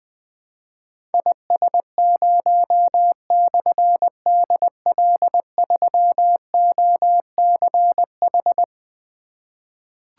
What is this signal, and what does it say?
Morse code, heard as a keyed tone, about 20 wpm, 700 Hz. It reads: IS0/DL3OCH
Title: Difficult Call Signs 20wpm